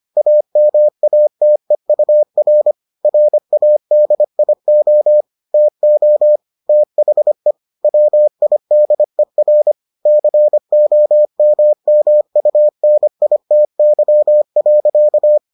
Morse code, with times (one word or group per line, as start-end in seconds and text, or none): 0.2-2.7 AMATEUR
3.0-5.2 RADIO
5.5-6.4 TO
6.7-7.5 THE
7.8-9.7 WIDER
10.1-15.4 COMMUNITY.